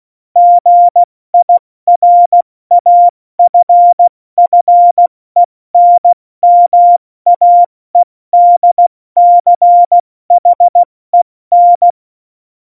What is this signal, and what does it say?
Morse code, heard as a keyed tone, about 16 wpm, 700 Hz. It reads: GIRAFFENMAEDCHEN
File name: Morsekurs / Witze / Witz 29. Dezember 2020, 70 F80